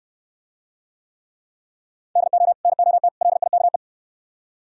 Morse code, heard as a keyed tone, about 69 wpm, 700 Hz. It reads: DO NOT LIKE